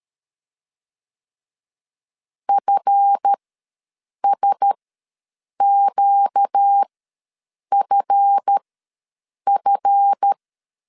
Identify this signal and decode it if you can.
Morse code, heard as a keyed tone, about 13 wpm, spaced as for 7 wpm, 800 Hz. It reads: FSQFF
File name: LEZIONI / 7 GF9Z / 7 GF ws 12wpm 3140